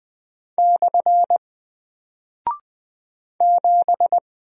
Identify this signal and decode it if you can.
Morse code, heard as a keyed tone, about 20 wpm, 700 Hz. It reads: / 7
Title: Single Character (Letter:Number:Punctuation:Prosign) 20wpm